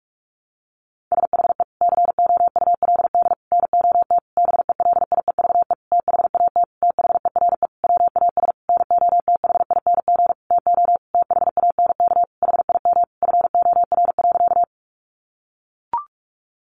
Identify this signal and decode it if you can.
Morse code, heard as a keyed tone, about 44 wpm, 700 Hz. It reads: SHE COULD NOT BELIEVE THAT THERE WAS NOTHING TO THANK HIM FOR.
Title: Sentences from Top 1000 Words 45wpm